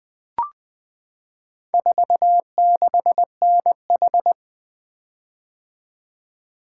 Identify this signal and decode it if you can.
Morse code, heard as a keyed tone, about 20 wpm, 700 Hz. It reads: 46NH
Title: Four Letter-Number Combinations - 20wpm